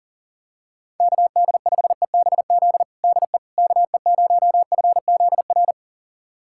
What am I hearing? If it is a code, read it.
Morse code, heard as a keyed tone, about 40 wpm, 700 Hz. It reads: KD5EBZ DE KE0FZR
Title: USA Callsign DE Callsign - 6 Characters - 40wpm